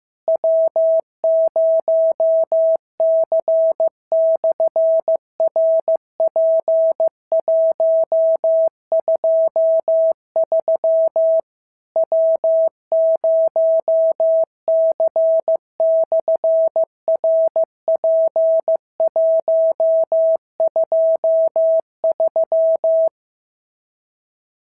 Morse code, heard as a keyed tone, about 15 wpm, 650 Hz. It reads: W0C/RP123 W0C/RP123
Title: SOTA QSO Practice 15wpm